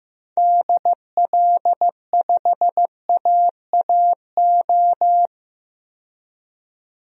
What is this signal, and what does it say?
Morse code, heard as a keyed tone, about 15 wpm, 700 Hz. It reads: DL5AAO
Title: DXCC-100 Call Signs - 6 Characters - 15wpm